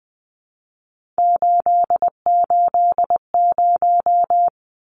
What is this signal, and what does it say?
Morse code, heard as a keyed tone, about 20 wpm, 700 Hz. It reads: 880
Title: Numbers - 3-Digit 20wpm